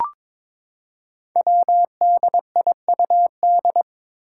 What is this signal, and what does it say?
Morse code, heard as a keyed tone, about 22 wpm, 700 Hz. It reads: WDIUD
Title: Code Groups - 5 Letter:Numbers - 22wpm